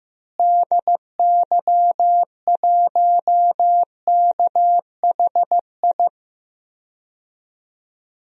Morse code, heard as a keyed tone, about 15 wpm, 700 Hz. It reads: DY1KHI
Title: DXCC-100 Call Signs - 6 Characters - Repeated 2x - 15wpm